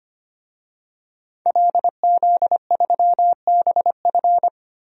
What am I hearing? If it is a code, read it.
Morse code, heard as a keyed tone, about 25 wpm, 700 Hz. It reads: LZ3BF
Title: DXCC-100 Call Signs - 5 Characters - Repeated 3x - 25wpm